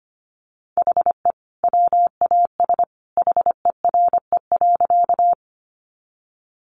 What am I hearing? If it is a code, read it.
Morse code, heard as a keyed tone, about 25 wpm, 700 Hz. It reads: HE WAS HERE.
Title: Sentences from Top 200 Words 25wpm